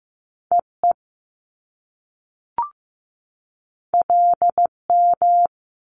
Morse code, heard as a keyed tone, about 15 wpm, 700 Hz. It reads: EE LM
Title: Top Two Letter Combinations in Words 15wpm